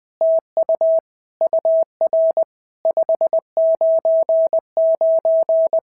Morse code, heard as a keyed tone, about 20 wpm, 650 Hz. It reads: TU UR 599
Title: POTA QSO Practice - 20wpm